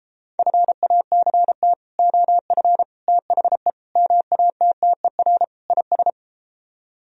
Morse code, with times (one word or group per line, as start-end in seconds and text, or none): 0.4-1.7 FACT
2.0-2.8 OF
3.1-3.7 THE
4.0-5.4 MATTER
5.7-6.1 IS